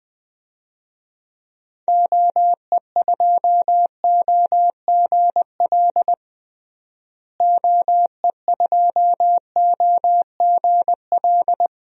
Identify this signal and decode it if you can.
Morse code, heard as a keyed tone, about 20 wpm, 700 Hz. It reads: OE2OGL OE2OGL
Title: DXCC-100 Call Signs - 6 Characters - Repeated 3x - 20wpm